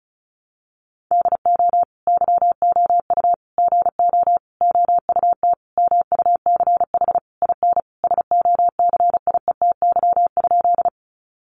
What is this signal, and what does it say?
Morse code, heard as a keyed tone, about 34 wpm, 700 Hz. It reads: DO YOU GO OUT MUCH IN SOCIETY?
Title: Sentences from Top 700 Words 35wpm